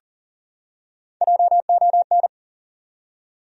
Morse code, heard as a keyed tone, about 40 wpm, 700 Hz. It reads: JON
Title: USA Common Names - Max 4 Letters - 40wpm